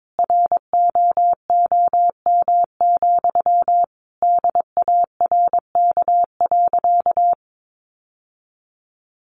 Morse code, heard as a keyed tone, about 22 wpm, 700 Hz. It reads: ROOM, DARK.
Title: Sentences from Top 1400 Words 22wpm